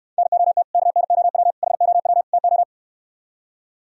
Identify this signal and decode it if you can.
Morse code, heard as a keyed tone, about 68 wpm, 700 Hz. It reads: NOT KNOW HOW TO